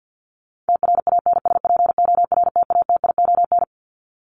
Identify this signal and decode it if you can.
Morse code, heard as a keyed tone, about 50 wpm, 700 Hz. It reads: TRANSPORTATION
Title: Top 1501-1750 Words 50wpm